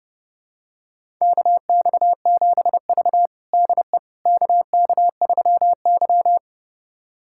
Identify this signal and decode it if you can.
Morse code, heard as a keyed tone, about 30 wpm, 700 Hz. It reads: KX7V DE KK3Y
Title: USA Callsign DE Callsign - 4 Characters - 30wpm